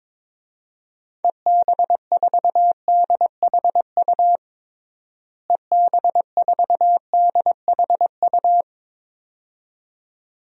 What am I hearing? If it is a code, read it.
Morse code, heard as a keyed tone, about 22 wpm, 700 Hz. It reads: EB4DHU EB4DHU
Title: DXCC-100 Call Signs - 6 Characters - Repeated 2x - 22wpm